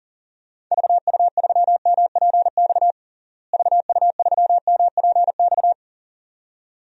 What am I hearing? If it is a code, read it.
Morse code, heard as a keyed tone, about 40 wpm, 700 Hz. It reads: VU3MPX VU3MPX
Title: DXCC-100 Call Signs - 6 Characters - Repeated 2x - 40wpm